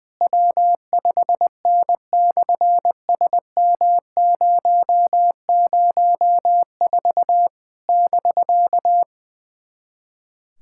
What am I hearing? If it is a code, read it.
Morse code, heard as a keyed tone, about 20 wpm, 700 Hz. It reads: W5N/SM004 <BK>
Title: SOTA QSO Practice 20wpm